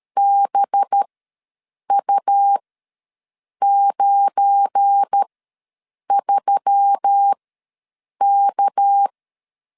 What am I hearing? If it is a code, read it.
Morse code, heard as a keyed tone, about 13 wpm, spaced as for 7 wpm, 800 Hz. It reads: BU93K
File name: LEZIONI / 7 GF9Z / 7 GF9Z ws 12wpm 3321